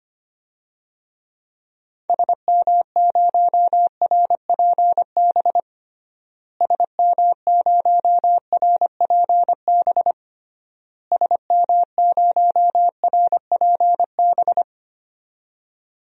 Morse code, heard as a keyed tone, about 25 wpm, 700 Hz. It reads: SM0RPB SM0RPB SM0RPB
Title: DXCC-100 Call Signs - 6 Characters - Repeated 3x - 25wpm